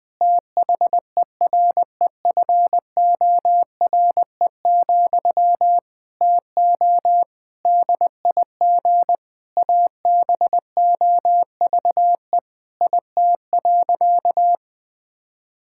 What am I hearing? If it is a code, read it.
Morse code, heard as a keyed tone, about 20 wpm, 700 Hz. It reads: THEREFORE, TO DIG ABOVE IT.
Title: Sentences from Top 1600 Words 20wpm